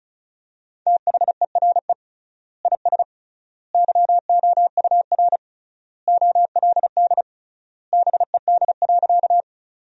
Morse code, heard as a keyed tone, about 34 wpm, 700 Hz. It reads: THERE IS YOUR OLD BED.
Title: 3x Word-Spacing - Sentences from Top 500 Words - 35wpm